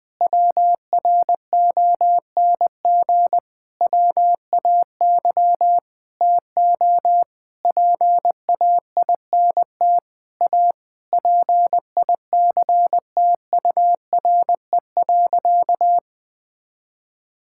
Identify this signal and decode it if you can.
Morse code, heard as a keyed tone, about 20 wpm, 700 Hz. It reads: WRONG WAY TO PAINT A PICTURE.
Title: Sentences with 8-Word Phrases - 20wpm